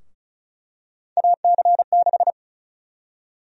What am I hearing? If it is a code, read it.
Morse code, heard as a keyed tone, about 36 wpm, 700 Hz. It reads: AC6
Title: USA Call Sign Prefixes - Repeated 2x - 35wpm